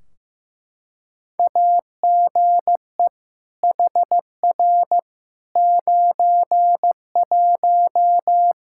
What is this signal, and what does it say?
Morse code, heard as a keyed tone, about 15 wpm, 700 Hz. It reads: AGE HR 91
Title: Practice QSO - Exchange 3 - More Info - 15wpm